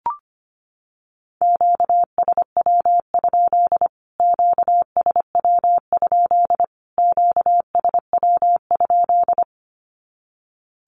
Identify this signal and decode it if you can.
Morse code, heard as a keyed tone, about 25 wpm, 700 Hz. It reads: QSW? QSW? QSW?
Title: Q-Codes Repeated 3x 25wpm